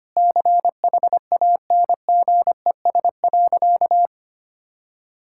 Morse code, heard as a keyed tone, about 25 wpm, 700 Hz. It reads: CHANGES.
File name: practice / 25wpm / Sentences with 3-Word Phrases - 25wpm